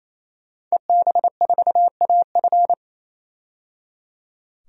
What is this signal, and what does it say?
Morse code, heard as a keyed tone, about 28 wpm, 700 Hz. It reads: EB4AF
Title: DXCC-100 Call Signs - 5 Characters - 28wpm